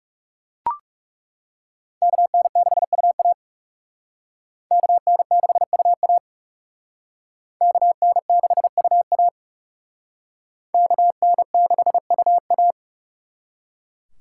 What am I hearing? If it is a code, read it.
Morse code, heard as a keyed tone, about 38 wpm, 700 Hz. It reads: KN6UA KN6UA KN6UA KN6UA
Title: Speed Racing - USA Call Signs 30wpm